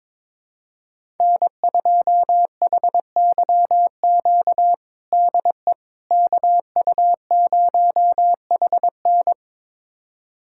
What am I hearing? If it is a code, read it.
Morse code, heard as a keyed tone, about 21 wpm, 700 Hz. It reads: N2HYQ DE KU0HN
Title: USA Callsign DE Callsign - 5 Characters - 22wpm